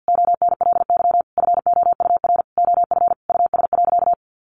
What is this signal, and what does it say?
Morse code, heard as a keyed tone, about 50 wpm, 700 Hz. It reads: ONLY FOUR OF US.
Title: Sentences from Top 300 Words 50wpm